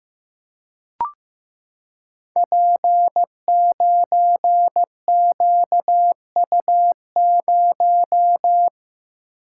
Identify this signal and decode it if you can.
Morse code, heard as a keyed tone, about 15 wpm, 700 Hz. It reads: P9QU0
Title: Code Groups - 5 Letter:Numbers - 15wpm